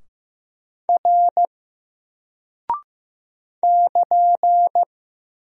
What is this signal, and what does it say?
Morse code, heard as a keyed tone, about 15 wpm, 700 Hz. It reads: R <KN>
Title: Single Character (Letter:Number:Punctuation:Prosign) 15wpm